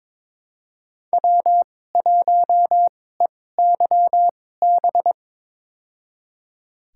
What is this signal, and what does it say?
Morse code, heard as a keyed tone, about 22 wpm, spaced as for 16 wpm, 700 Hz. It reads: W1EYB
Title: USA Call Signs - 5 Characters - 2x Character Spacing - 22wpm